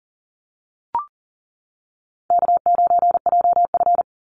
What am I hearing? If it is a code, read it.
Morse code, heard as a keyed tone, about 38 wpm, 700 Hz. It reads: K9JF
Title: Easy Call Signs - 40wpm